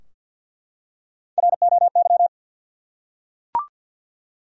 Morse code, heard as a keyed tone, about 52 wpm, 700 Hz. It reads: ROY
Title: USA Common Names - Max 4 Letters - 50wpm